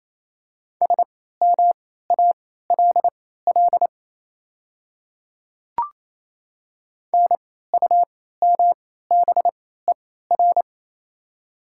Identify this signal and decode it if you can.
Morse code, heard as a keyed tone, about 28 wpm, spaced as for 16 wpm, 700 Hz. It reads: SMALL NUMBER
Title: 3x Character Spacing - Top 200 Words - 28wpm